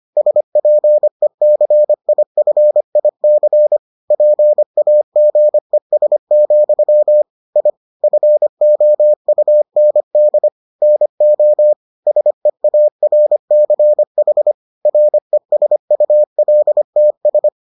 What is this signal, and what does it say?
Morse code, heard as a keyed tone, about 25 wpm, 600 Hz. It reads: SPECIFIC PAGES, I FOUND NO SEARCH RESULTS